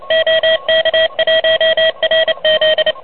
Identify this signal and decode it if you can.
Morse code, heard as a keyed tone, about 31 wpm, 590 Hz, with noise. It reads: OK1RZ
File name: truSDX / MWC29.8